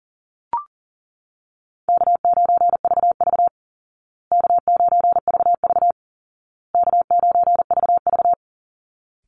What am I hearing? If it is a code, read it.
Morse code, heard as a keyed tone, about 40 wpm, 700 Hz. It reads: K9VV K9VV K9VV
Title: Call Signs Repeated 2x - 40wpm